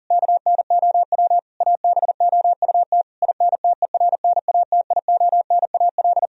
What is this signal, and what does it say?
Morse code, heard as a keyed tone, about 40 wpm, 700 Hz. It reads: KNOW ABOUT INTERNATIONAL
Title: Sentences from Top 1000 Words - Review Part 4 - 40wpm